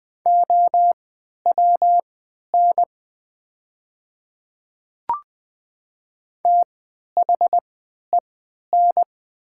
Morse code, heard as a keyed tone, about 20 wpm, spaced as for 11 wpm, 700 Hz. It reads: OWN THEN
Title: 3x Character Spacing - Top 300 Words - 20wpm